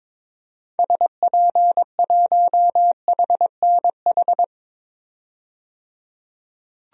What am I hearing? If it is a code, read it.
Morse code, heard as a keyed tone, about 22 wpm, 700 Hz. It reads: SP1HNH
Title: DXCC-100 Call Signs - 6 Characters - 22wpm